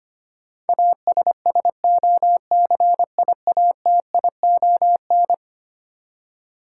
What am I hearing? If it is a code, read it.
Morse code, heard as a keyed tone, about 25 wpm, 700 Hz. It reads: ASSOCIATION
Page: morse-code-ninja-practice-25wpm